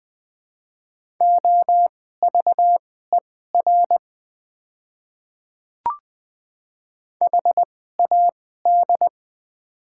Morse code, heard as a keyed tone, about 21 wpm, spaced as for 15 wpm, 700 Hz. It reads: OVER HAD
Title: 2x Character Spacing - Top 100 Words - 20wpm